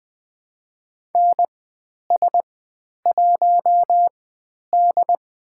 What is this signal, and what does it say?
Morse code, heard as a keyed tone, about 20 wpm, spaced as for 10 wpm, 700 Hz. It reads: NS1D
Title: 102 Lesson 17g - Letter u - Call Signs 20wpm